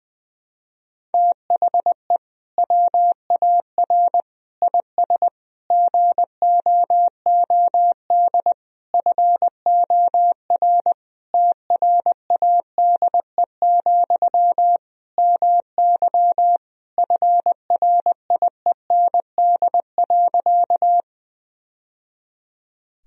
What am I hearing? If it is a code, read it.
Morse code, heard as a keyed tone, about 20 wpm, 700 Hz. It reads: THE WAR IS GOOD FOR TRADE, MY FRIEND.
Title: Sentences from Top 1000 Words - Review Part 4 - 20wpm